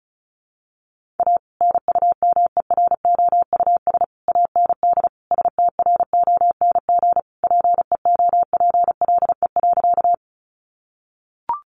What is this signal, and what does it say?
Morse code, heard as a keyed tone, about 34 wpm, 700 Hz. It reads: A NUMEROUS AND STRONG PEOPLE.
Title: Sentences from Top 1400 Words 35wpm